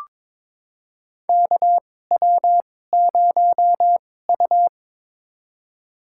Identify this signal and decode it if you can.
Morse code, heard as a keyed tone, about 22 wpm, spaced as for 16 wpm, 700 Hz. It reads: KW0U
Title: USA Call Signs - 4 Characters - 2x Character Spacing - 22wpm